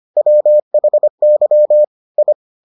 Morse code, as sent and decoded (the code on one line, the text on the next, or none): .-- .... -.-- / ..
WHY I